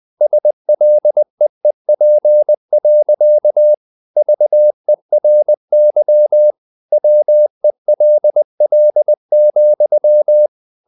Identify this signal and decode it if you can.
Morse code, heard as a keyed tone, about 20 wpm, 600 Hz. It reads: SLEEP. VERY WELL,
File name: the old man and the sea / the old man and the sea part 1 20wpm